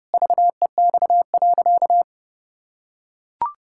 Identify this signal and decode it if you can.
Morse code, heard as a keyed tone, about 30 wpm, 700 Hz. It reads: VEX.
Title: Pangram Challenge v2 - 30wpm